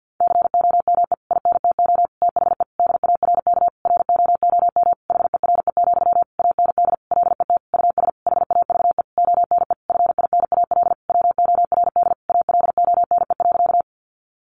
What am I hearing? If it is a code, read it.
Morse code, heard as a keyed tone, about 50 wpm, 700 Hz. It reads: COME INTO THE DARK ROOM HERE, AND LET US HAVE ONE FINAL WORD ALONE.
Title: Sentences from Top 1000 Words 50wpm